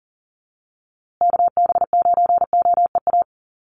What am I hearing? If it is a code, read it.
Morse code, heard as a keyed tone, about 40 wpm, 700 Hz. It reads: KB9OEA
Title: USA Call Signs Repeated 2x 40wpm